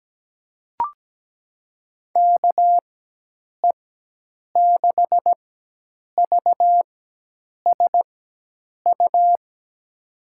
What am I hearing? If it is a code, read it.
Morse code, heard as a keyed tone, about 17 wpm, spaced as for 8 wpm, 700 Hz. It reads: KE6VSU
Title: USA Call Signs - 4x Character Spacing - 17wpm